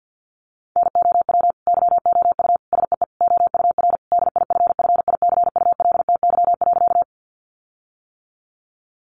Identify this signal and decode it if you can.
Morse code, heard as a keyed tone, about 50 wpm, 700 Hz. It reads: NOW YOU SEE OUR DIFFICULTY.
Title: Sentences from Top 1500 Words 50wpm